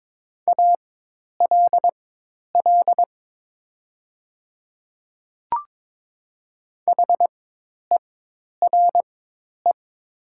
Morse code, heard as a keyed tone, about 22 wpm, spaced as for 10 wpm, 700 Hz. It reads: ALL HERE
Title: 4x Character Spacing - Top 200 Words - 22wpm